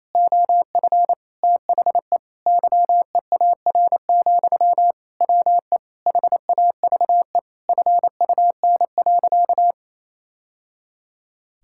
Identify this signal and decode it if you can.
Morse code, heard as a keyed tone, about 27 wpm, 700 Hz. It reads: OF THE YEAR, WE HAVE FUN.
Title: Sentences from Top 1000 Words - Review Part 5 - 28wpm